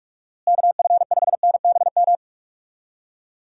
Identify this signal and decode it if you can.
Morse code, heard as a keyed tone, about 46 wpm, 700 Hz. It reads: KF5NBM
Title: USA Call Signs - 6 Characters - Repeated 2x - 45wpm